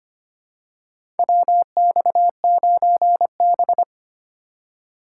Morse code, heard as a keyed tone, about 25 wpm, 700 Hz. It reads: WX9B
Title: USA Call Signs - 4 Characters - 3x Character Spacing - 25wpm